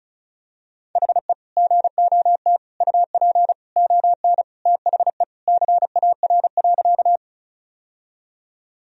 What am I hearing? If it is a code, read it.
Morse code, heard as a keyed tone, about 36 wpm, 700 Hz. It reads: HE GOT UP ON THE CAR.
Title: Sentences from Top 1000 Words - Review Part 1 - 35wpm